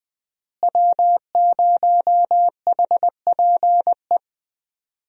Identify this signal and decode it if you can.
Morse code, heard as a keyed tone, about 21 wpm, 700 Hz. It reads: W0HPE
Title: USA Call Signs - 5 Characters - Repeated 3x - 20wpm